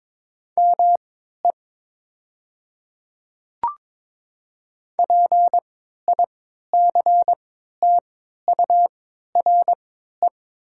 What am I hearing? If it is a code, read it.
Morse code, heard as a keyed tone, about 22 wpm, spaced as for 12 wpm, 700 Hz. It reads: ME PICTURE
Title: 3x Character Spacing - Top 200 Words - 22wpm